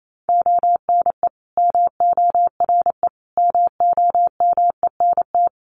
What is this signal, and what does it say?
Morse code, heard as a keyed tone, about 28 wpm, 700 Hz. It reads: ONE MORE MOMENT
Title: Sentences from Top 1300 Words 28wpm